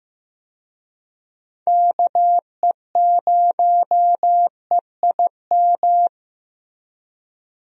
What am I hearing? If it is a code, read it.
Morse code, heard as a keyed tone, about 15 wpm, 700 Hz. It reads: KE0EIM
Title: USA Call Signs - 6 Characters - 15wpm